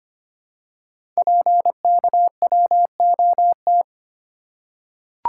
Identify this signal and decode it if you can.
Morse code, heard as a keyed tone, about 25 wpm, 700 Hz. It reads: PKWOT
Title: Code Groups - 5 Letters - 25wpm